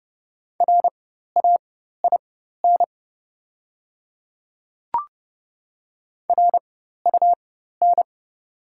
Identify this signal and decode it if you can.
Morse code, heard as a keyed tone, about 31 wpm, spaced as for 14 wpm, 690 Hz. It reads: RAIN RUN
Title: 4x Character Spacing - Top 300 Words - 30wpm